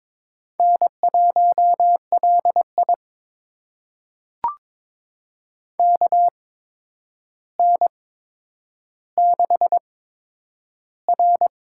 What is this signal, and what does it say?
Morse code, heard as a keyed tone, about 22 wpm, spaced as for 9 wpm, 700 Hz. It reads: N1LI KN6R
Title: USA Call Signs - 4 Characters - 8x Character Spacing - 22wpm